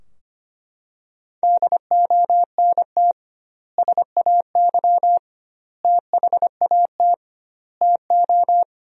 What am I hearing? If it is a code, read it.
Morse code, heard as a keyed tone, about 25 wpm, 700 Hz. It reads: DONT SAY THAT TO
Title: 2x Word-Spacing - Sentences from Top 300 Words - 25wpm